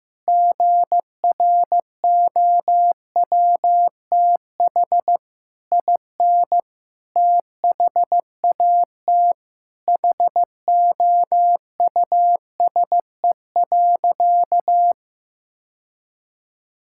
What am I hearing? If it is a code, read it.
Morse code, heard as a keyed tone, about 14 wpm, 700 Hz. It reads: GROWTH IN THAT HOUSE.
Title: Sentences from Top 1000 Words - Review Part 5 - 15wpm